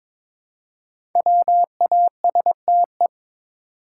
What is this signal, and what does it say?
Morse code, heard as a keyed tone, about 22 wpm, 700 Hz. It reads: WASTE
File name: 5 Letter Words - First 3 Letters Given - 22wpm